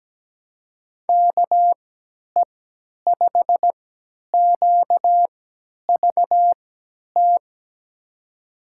Morse code, heard as a keyed tone, about 17 wpm, spaced as for 10 wpm, 700 Hz. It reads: KE5QVT
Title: USA Call Signs - 6 Characters - 3x Character Spacing - 17wpm